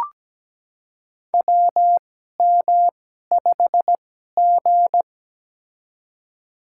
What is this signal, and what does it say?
Morse code, heard as a keyed tone, about 17 wpm, spaced as for 12 wpm, 700 Hz. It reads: WM5G
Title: USA Call Signs - 4 Characters - 2x Character Spacing - 17wpm